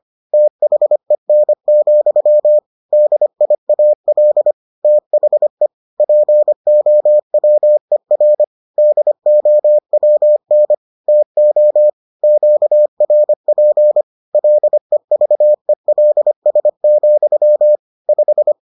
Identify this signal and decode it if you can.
Morse code, heard as a keyed tone, about 25 wpm, 600 Hz. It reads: THEN, DIAL THE POWER DOWN TO QRP LEVELS, 5